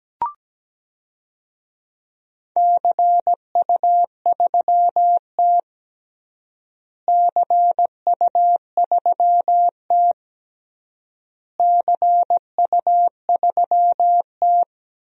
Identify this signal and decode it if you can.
Morse code, heard as a keyed tone, about 17 wpm, 700 Hz. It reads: CU3T CU3T CU3T
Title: DXCC-100 Call Signs - 4 Characters - Repeated 3x - 17wpm